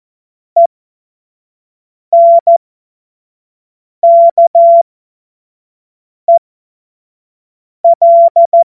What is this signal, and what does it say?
Morse code, heard as a keyed tone, about 14 wpm, spaced as for 5 wpm, 680 Hz. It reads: ENKEL
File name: Morsekurs / Witze / Witz 26. Oktober 2021, 25 F70